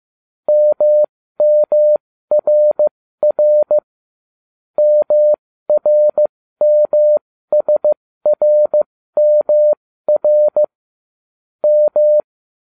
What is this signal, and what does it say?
Morse code, heard as a keyed tone, about 15 wpm, spaced as for 13 wpm, 600 Hz. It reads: MMRR MRMSRMR M